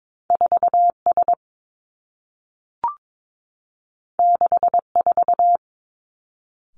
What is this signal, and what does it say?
Morse code, heard as a keyed tone, about 22 wpm, 700 Hz. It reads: 4 S 6 4
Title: Two Challenging Letter-Number Combinations 22wpm